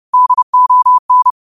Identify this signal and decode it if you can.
Morse code, heard as a keyed tone, about 30 wpm, 1000 Hz. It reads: DON
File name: qotdfiles / 30wpm